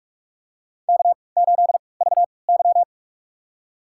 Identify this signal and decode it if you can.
Morse code, heard as a keyed tone, about 46 wpm, spaced as for 26 wpm, 700 Hz. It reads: K8VY
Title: USA Call Signs - 4 Characters - 3x Character Spacing - 45wpm